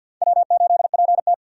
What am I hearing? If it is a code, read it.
Morse code, heard as a keyed tone, about 48 wpm, 700 Hz. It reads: W8PT
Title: USA Call Signs - 4 Characters - 50wpm